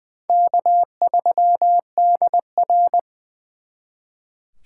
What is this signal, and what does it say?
Morse code, heard as a keyed tone, about 23 wpm, 700 Hz. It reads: K3DR
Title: Speed Racing - USA 1x2 Call Signs 20wpm